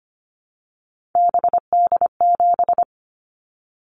Morse code, heard as a keyed tone, about 25 wpm, 700 Hz. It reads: BD7
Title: Callsign Prefixes for Top 100 DXCC - 25wpm